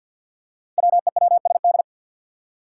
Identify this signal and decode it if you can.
Morse code, heard as a keyed tone, about 46 wpm, 700 Hz. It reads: WEWSD
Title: Five Challenging Letter-Number Combinations 50wpm